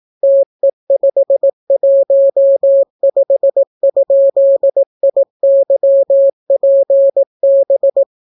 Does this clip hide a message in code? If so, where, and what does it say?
Yes: Morse code from 0.2 to 8.0 s, heard as a keyed tone, about 18 wpm, 550 Hz. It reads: TE515?IYPB